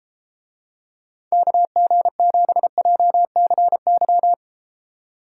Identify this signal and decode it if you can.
Morse code, heard as a keyed tone, about 32 wpm, 700 Hz. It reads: KG7JCY